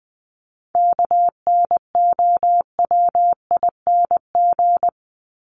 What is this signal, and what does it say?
Morse code, heard as a keyed tone, about 20 wpm, 700 Hz. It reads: KNOWING
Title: ING Suffix - Top 100 Words with Base Words 20wpm